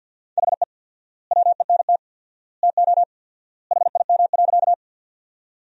Morse code, heard as a keyed tone, about 50 wpm, 700 Hz. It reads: HE WENT TO HIM.
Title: 4x Word-Spacing - Sentences from Top 200 Words - 50wpm